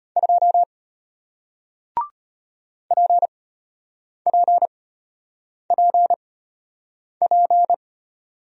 Morse code, heard as a keyed tone, about 33 wpm, spaced as for 8 wpm, 700 Hz. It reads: 2 PPPP